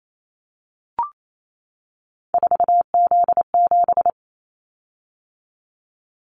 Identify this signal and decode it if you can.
Morse code, heard as a keyed tone, about 28 wpm, 700 Hz. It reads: 4Z7
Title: Callsign Prefixes for Top 50 DXCC - 28wpm